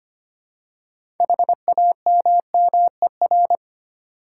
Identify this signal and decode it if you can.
Morse code, heard as a keyed tone, about 25 wpm, 700 Hz. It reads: HAMMER
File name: practice / 25wpm / Easy Words with Hints - 25wpm